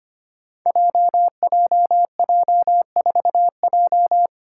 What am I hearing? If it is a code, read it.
Morse code, heard as a keyed tone, about 25 wpm, 700 Hz. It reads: JJJ4J
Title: Five Challenging Letter-Number Combinations 25wpm